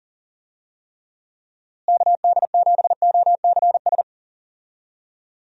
Morse code, heard as a keyed tone, about 40 wpm, 700 Hz. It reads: KD7OCS
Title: USA Call Signs - 6 Characters - Repeated 3x - 40wpm